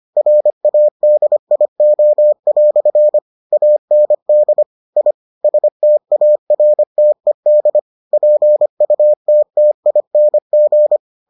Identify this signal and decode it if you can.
Morse code, heard as a keyed tone, about 25 wpm, 600 Hz. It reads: RADIO" AND I STARTED PUTTING